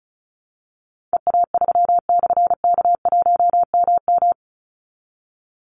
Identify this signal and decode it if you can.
Morse code, heard as a keyed tone, about 34 wpm, 700 Hz. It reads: EA3/K1MM
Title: Call Signs - 35wpm